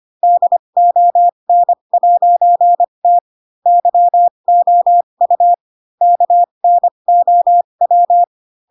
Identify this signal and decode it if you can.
Morse code, heard as a keyed tone, about 25 wpm, spaced as for 22 wpm, 700 Hz. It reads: DON'T YOU KNOW
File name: Pinocchio / Pinocchio ch4 18 25 ik1hge